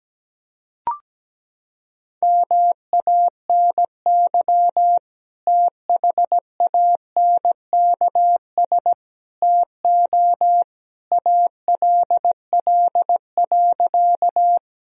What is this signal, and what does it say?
Morse code, heard as a keyed tone, about 17 wpm, 700 Hz. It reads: MANY THANKS TO ALL.
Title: Sentences from Top 1000 Words - Review Part 3 - 17wpm